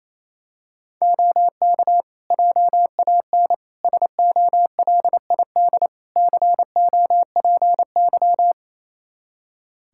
Morse code, heard as a keyed tone, about 28 wpm, 700 Hz. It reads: OK JAN SOLID COPY
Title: K1USN SST QSO Practice 28wpm